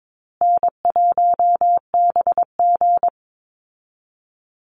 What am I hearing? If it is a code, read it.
Morse code, heard as a keyed tone, about 22 wpm, 700 Hz. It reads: N1BG
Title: Easy Call Signs Repeated 3x - 22wpm